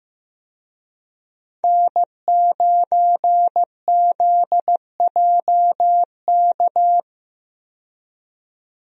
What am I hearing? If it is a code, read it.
Morse code, heard as a keyed tone, about 15 wpm, 700 Hz. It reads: N9ZJK
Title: USA Call Signs - 5 Characters - 15wpm